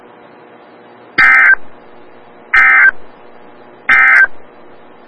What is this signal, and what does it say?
Emergency Alert System (SAME) tones, heard as harsh buzzing bursts. It is heard from 1.1 to 4.4 s.
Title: Des Moines, IA NOAA Weather Radio - WXL57